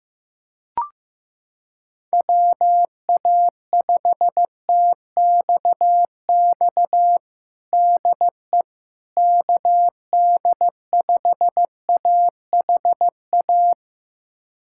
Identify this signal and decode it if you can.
Morse code, heard as a keyed tone, about 15 wpm, 700 Hz. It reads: WA5TXX DE KD5AHA
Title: USA Callsign DE Callsign - 6 Characters - 15wpm